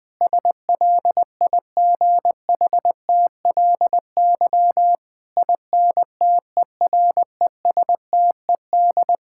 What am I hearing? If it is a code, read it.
Morse code, heard as a keyed tone, about 20 wpm, 700 Hz. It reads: SLIGHTLY INTERESTED
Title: Sentences from Top 1600 Words 20wpm